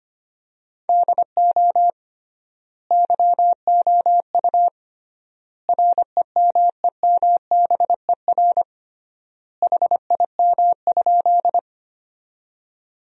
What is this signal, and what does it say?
Morse code, heard as a keyed tone, about 25 wpm, 700 Hz. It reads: DO YOU REMEMBER HIM?
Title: 3x Word-Spacing - Sentences from Top 400 Words - 25wpm